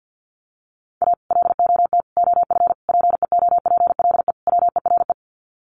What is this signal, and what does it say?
Morse code, heard as a keyed tone, about 50 wpm, 700 Hz. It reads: A LOT OF PEOPLE WERE
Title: Sets of 5 Words Encore2 50wpm